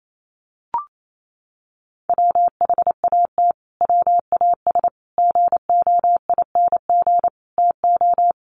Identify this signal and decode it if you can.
Morse code, heard as a keyed tone, about 28 wpm, 700 Hz. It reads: WHAT WAS GOING TO
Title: Sets of 5 Words Encore 28wpm